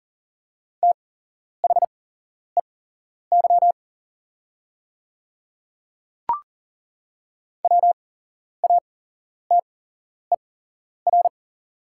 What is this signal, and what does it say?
Morse code, heard as a keyed tone, about 40 wpm, spaced as for 11 wpm, 700 Hz. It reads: THEY WATER
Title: 8x Character Spacing - Top 100 Words - 40wpm